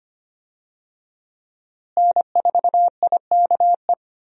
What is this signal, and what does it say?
Morse code, heard as a keyed tone, about 25 wpm, 700 Hz. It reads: N4IKE
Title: USA Call Signs - 5 Characters - Repeated 3x - 25wpm